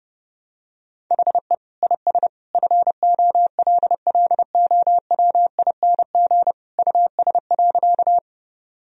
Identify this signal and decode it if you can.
Morse code, heard as a keyed tone, about 32 wpm, 700 Hz. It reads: HE IS FOLLOWING US.
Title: Sentences from Top 1000 Words - Review Part 4 - 30wpm